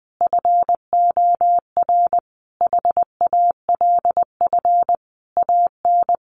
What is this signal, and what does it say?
Morse code, heard as a keyed tone, about 20 wpm, 700 Hz. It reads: FOR HALF AN
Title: Sentences from Top 1100 Words 20wpm